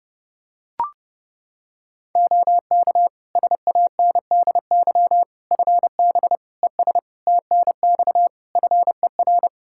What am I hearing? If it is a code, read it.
Morse code, heard as a keyed tone, about 30 wpm, 700 Hz. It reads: OK SANDY FB ES TNX FER
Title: K1USN SST QSO Practice 30wpm